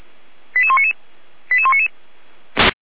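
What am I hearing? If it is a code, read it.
Five-tone selective call (ZVEI), heard as quick runs of five tones, tones of about 70 ms. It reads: 88199 88199
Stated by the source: Five-tone ZVEI selective call 89010, received by SDR, from fire brigade Orlamünde (feuerwehr-orlamuende).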